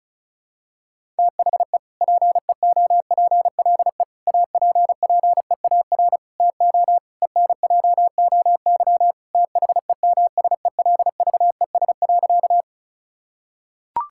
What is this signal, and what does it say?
Morse code, heard as a keyed tone, about 34 wpm, 700 Hz. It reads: THE PEOPLE APPEAR TO ENJOY THEMSELVES.
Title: Sentences from Top 1000 Words - Review Part 5 - 35wpm